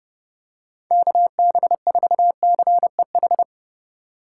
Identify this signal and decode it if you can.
Morse code, heard as a keyed tone, about 29 wpm, 700 Hz. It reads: KB4CEH
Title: USA Call Signs - 6 Characters - 4x Character Spacing - 30wpm